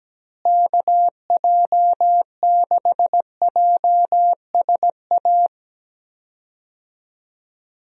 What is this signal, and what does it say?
Morse code, heard as a keyed tone, about 17 wpm, 700 Hz. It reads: KJ6JSA
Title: USA Call Signs - 6 Characters - Repeated 3x - 17wpm